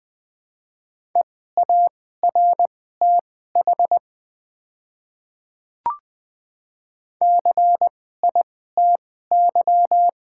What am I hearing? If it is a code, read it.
Morse code, heard as a keyed tone, about 20 wpm, spaced as for 14 wpm, 700 Hz. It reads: EARTH CITY
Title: 2x Character Spacing - Top 300 Words - 20wpm